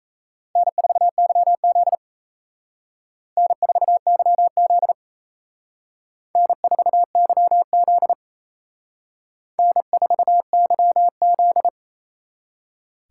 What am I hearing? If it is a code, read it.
Morse code, heard as a keyed tone, about 35 wpm, 700 Hz. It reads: N4YZ N4YZ N4YZ N4YZ
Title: Speed Racing - USA 1x2 Call Signs 28wpm